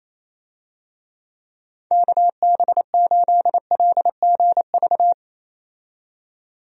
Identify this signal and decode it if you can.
Morse code, heard as a keyed tone, about 28 wpm, 700 Hz. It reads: KB8LGV
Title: USA Call Signs - 6 Characters - 28wpm